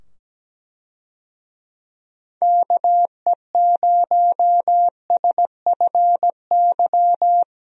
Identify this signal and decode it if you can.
Morse code, heard as a keyed tone, about 17 wpm, 700 Hz. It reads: KE0SFY
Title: USA Call Signs - 6 Characters - 17wpm